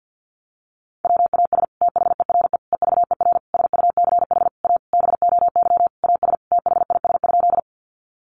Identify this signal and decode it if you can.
Morse code, heard as a keyed tone, about 50 wpm, 700 Hz. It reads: WAS THERE EVER SUCH A BOY AS THIS?
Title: Sentences from Top 400 Words 50wpm